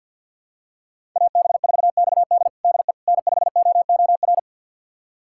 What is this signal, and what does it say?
Morse code, heard as a keyed tone, about 48 wpm, 700 Hz. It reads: AB4XD DE N5OOR
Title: USA Callsign DE Callsign - 5 Characters - 50wpm